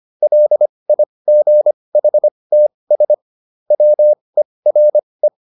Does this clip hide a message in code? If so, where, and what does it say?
Yes: Morse code from 0.2 to 5.3 s, heard as a keyed tone, about 25 wpm, spaced as for 20 wpm, 600 Hz. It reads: LIGHTS WERE